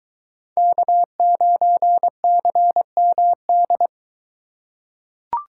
Morse code, heard as a keyed tone, about 22 wpm, 700 Hz. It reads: K9CMD